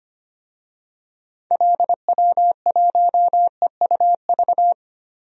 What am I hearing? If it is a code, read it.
Morse code, heard as a keyed tone, about 25 wpm, 700 Hz. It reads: LW1EUV
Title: DXCC-100 Call Signs - 6 Characters - 25wpm